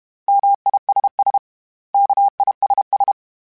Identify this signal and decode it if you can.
Morse code, heard as a keyed tone, about 32 wpm, 800 Hz. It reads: MISS KISS
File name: General / General-mp3s / Suffixes @30